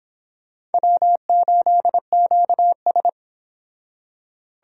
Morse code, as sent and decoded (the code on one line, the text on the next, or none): .-- ---.. --.- ...
W8QS